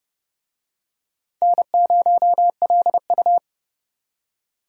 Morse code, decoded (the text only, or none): N0LU